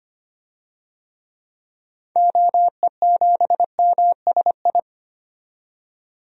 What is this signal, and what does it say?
Morse code, heard as a keyed tone, about 25 wpm, 700 Hz. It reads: OE7MSI
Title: DXCC-100 Call Signs - 6 Characters - 25wpm